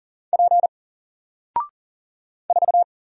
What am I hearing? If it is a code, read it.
Morse code, heard as a keyed tone, about 40 wpm, 700 Hz. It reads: P 4
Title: Single Letter-Number 40wpm